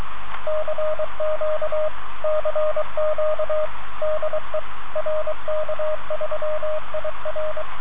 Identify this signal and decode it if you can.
Morse code, heard as a keyed tone, about 24 wpm, 620 Hz, with radio noise. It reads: CQ CQ DE RK3IR